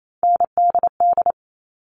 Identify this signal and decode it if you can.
Morse code, heard as a keyed tone, about 28 wpm, 700 Hz. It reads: NDD
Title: All Cut Numbers - 3-Digit 28wpm